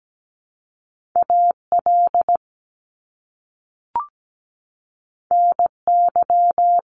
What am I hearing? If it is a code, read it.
Morse code, heard as a keyed tone, about 17 wpm, 700 Hz. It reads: AL NY
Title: Top Two Letter Combinations in Words 17wpm